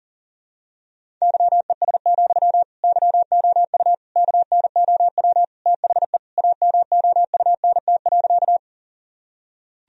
Morse code, heard as a keyed tone, about 37 wpm, 700 Hz. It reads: YES, YOU KNOW THE AMOUNT.
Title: Sentences from Top 1000 Words - Review Part 4 - 40wpm